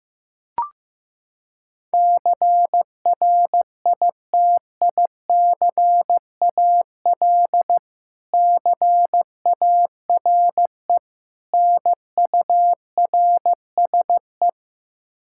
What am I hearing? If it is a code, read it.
Morse code, heard as a keyed tone, about 16 wpm, 700 Hz. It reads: CRITICAL CARE NURSE
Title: Practice QSO - Exchange 3 - More Info - 15wpm